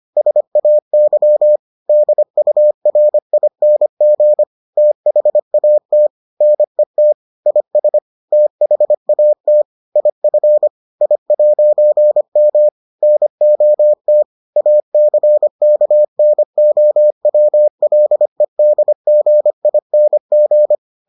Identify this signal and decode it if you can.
Morse code, heard as a keyed tone, about 25 wpm, 600 Hz. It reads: SAY DURING THAT NET IS THAT IF I'M NOT ACKNOWLEDGING